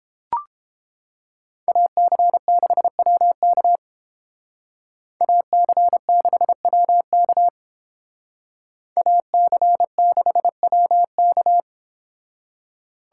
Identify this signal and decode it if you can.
Morse code, heard as a keyed tone, about 30 wpm, 700 Hz. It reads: AC6WK AC6WK AC6WK